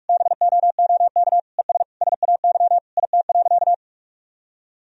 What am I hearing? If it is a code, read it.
Morse code, heard as a keyed tone, about 46 wpm, 700 Hz. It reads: BOOK ES SAY IT.
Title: Sentences from Top 400 Words and Abbreviations 45wpm